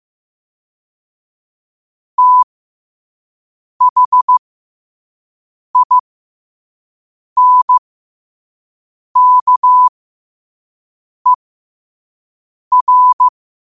Morse code, heard as a keyed tone, about 15 wpm, spaced as for 5 wpm, 1000 Hz. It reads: THINKER
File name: qotdfiles / 5wpm